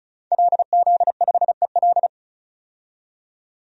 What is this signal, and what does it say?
Morse code, heard as a keyed tone, about 36 wpm, 700 Hz. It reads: LZ5EL
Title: DXCC-100 Call Signs - 5 Characters - 35wpm